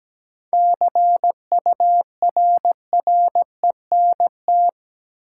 Morse code, heard as a keyed tone, about 17 wpm, 700 Hz. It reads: CURRENT
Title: Sentences from Top 1000 Words - Review Part 4 - New Word Drill - 17wpm